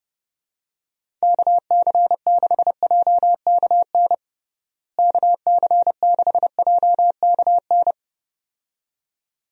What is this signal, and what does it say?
Morse code, heard as a keyed tone, about 30 wpm, 700 Hz. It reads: KC6JKN KC6JKN
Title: USA Call Signs - 6 Characters - Repeated 2x - 30wpm